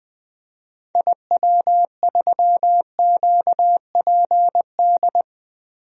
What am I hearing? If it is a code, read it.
Morse code, heard as a keyed tone, about 20 wpm, 700 Hz. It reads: IW3QPD
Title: DXCC-100 Call Signs - 6 Characters - Repeated 3x - 20wpm